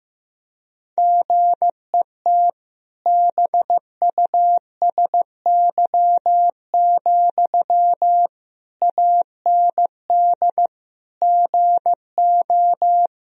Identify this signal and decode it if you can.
Morse code, heard as a keyed tone, about 15 wpm, 700 Hz. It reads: GET BUSY, AND GO
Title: Sentences from Top 1000 Words - Review Part 3 - 15wpm